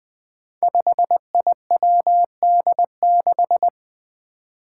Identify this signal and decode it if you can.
Morse code, heard as a keyed tone, about 21 wpm, 700 Hz. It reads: 5IWD6
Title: Five Challenging Letter-Number Combinations 20wpm